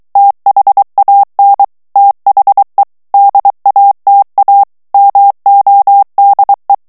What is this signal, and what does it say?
Morse code, heard as a keyed tone, about 23 wpm, 800 Hz. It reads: THAN THE DATA MODE
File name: ANGLAIS texte clair 3 vitesse 22 mots minute